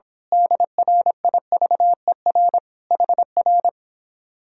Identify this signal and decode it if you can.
Morse code, heard as a keyed tone, about 26 wpm, 700 Hz. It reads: DRIVER HR
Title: Speed-Racing Common QSO Phrases - 22wpm